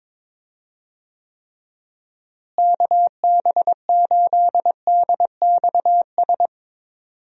Morse code, heard as a keyed tone, about 22 wpm, 700 Hz. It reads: KB8DXS
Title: USA Call Signs - 6 Characters - Repeated 2x - 22wpm